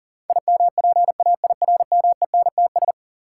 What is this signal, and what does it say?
Morse code, heard as a keyed tone, about 40 wpm, 700 Hz. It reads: IMPAIRMENTS
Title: Sets of 5 Words Encore - New Word Drill - 40wpm